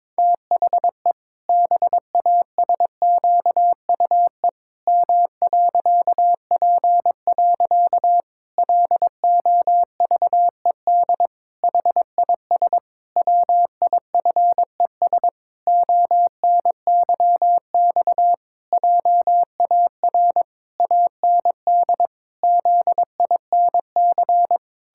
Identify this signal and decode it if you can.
Morse code, heard as a keyed tone, about 22 wpm, 700 Hz. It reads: THE BASQUE M.P. LOVED HIS WIFES ONYX JAR AND ZINC
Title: Pangram Challenge v2 - 22wpm